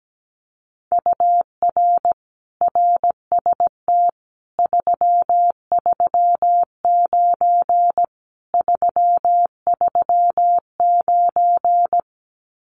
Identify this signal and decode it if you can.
Morse code, heard as a keyed tone, about 17 wpm, 700 Hz. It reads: UR RST 339 339
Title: Common QSO Phrases Encore 17wpm